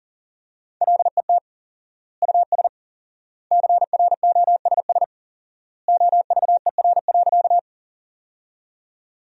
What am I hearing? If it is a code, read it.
Morse code, heard as a keyed tone, about 40 wpm, 700 Hz. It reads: LET US CROSS OVER.
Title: 4x Word-Spacing - Sentences from Top 300 Words - 40wpm